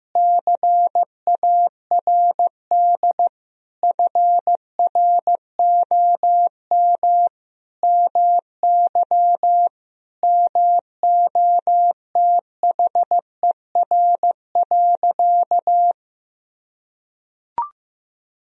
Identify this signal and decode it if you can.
Morse code, heard as a keyed tone, about 15 wpm, 700 Hz. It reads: CARD FROM MY MOTHER.
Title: Sentences from Top 1000 Words - Review Part 4 - 15wpm